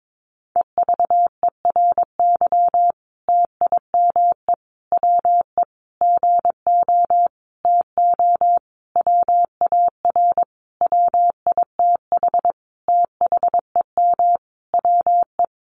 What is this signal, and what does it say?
Morse code, heard as a keyed tone, about 22 wpm, 700 Hz. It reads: EVERY TIME WE GO TO WAR WITH THEM WE
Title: Sentences from Top 1300 Words 22wpm